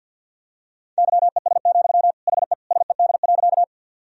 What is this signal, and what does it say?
Morse code, heard as a keyed tone, about 46 wpm, 700 Hz. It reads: YES, HE SED.
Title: Sentences from Top 500 Words and Abbreviations 50wpm